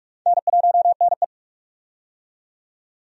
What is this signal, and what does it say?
Morse code, heard as a keyed tone, about 41 wpm, 700 Hz. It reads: N1NE
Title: USA Call Signs - 4 Characters - 45wpm